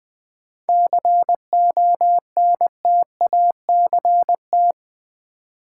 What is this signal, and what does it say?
Morse code, heard as a keyed tone, about 20 wpm, 700 Hz. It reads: CONTACT
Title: Sentences from Top 1000 Words - Review Part 7 - New Word Drill - 20wpm